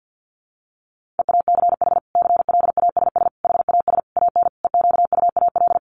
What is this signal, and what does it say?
Morse code, heard as a keyed tone, about 50 wpm, 700 Hz. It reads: EACH CLASS HAS AN EQUAL
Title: Sentences from Top 1400 Words 50wpm